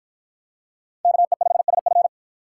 Morse code, heard as a keyed tone, about 50 wpm, 700 Hz. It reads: KE5SF